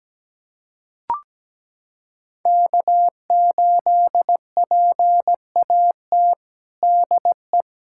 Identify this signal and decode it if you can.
Morse code, heard as a keyed tone, about 17 wpm, 700 Hz. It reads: K8PAT DE
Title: USA Callsign DE Callsign - 5 Characters - 17wpm